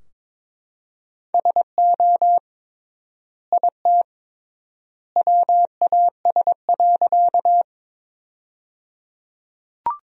Morse code, heard as a keyed tone, about 22 wpm, 700 Hz. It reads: SO IT WAS.
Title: 3x Word-Spacing - Sentences from Top 100 Words - 22wpm